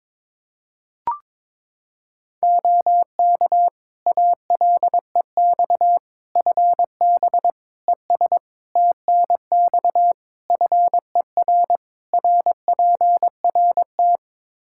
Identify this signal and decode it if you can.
Morse code, heard as a keyed tone, about 22 wpm, 700 Hz. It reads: OK ALEX FB ES TNX FER RPRT
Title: Practice QSO - Exchange 2 - Info - 22wpm